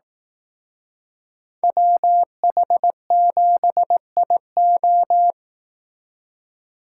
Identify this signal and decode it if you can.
Morse code, heard as a keyed tone, about 18 wpm, 700 Hz. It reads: WH7IO
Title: Speed Racing - USA 2x2 Call Signs 15wpm